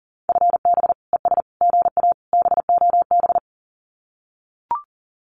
Morse code, heard as a keyed tone, about 40 wpm, 700 Hz. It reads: FB ES GA BOB
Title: Common QSO Phrases Encore 40wpm